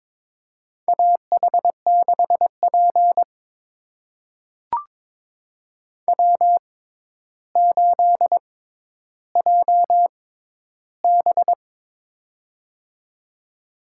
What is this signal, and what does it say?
Morse code, heard as a keyed tone, about 22 wpm, spaced as for 11 wpm, 700 Hz. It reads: AH6P W8JB
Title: USA Call Signs - 4 Characters - 6x Character Spacing - 22wpm